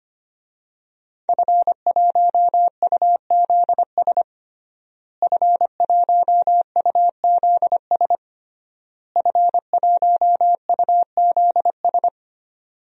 Morse code, heard as a keyed tone, about 25 wpm, 700 Hz. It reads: F1UZS F1UZS F1UZS
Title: DXCC-100 Call Signs - 5 Characters - Repeated 3x - 25wpm